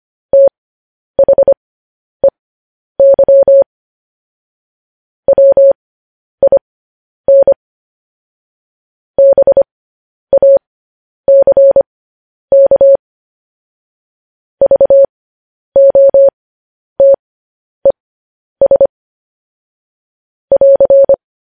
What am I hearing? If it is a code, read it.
Morse code, heard as a keyed tone, about 25 wpm, spaced as for 10 wpm, 550 Hz. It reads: THEY WIN BACK VOTES <AR>
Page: News Headlines in Morse Code at 10 WPM